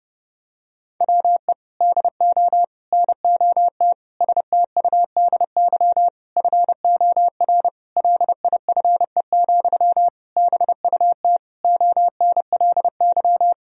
Morse code, heard as a keyed tone, about 30 wpm, 700 Hz. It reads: WE DO NOT STUDY FOR LIFE, BUT ONLY
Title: Sentences from Top 1000 Words - Review Part 1 - 30wpm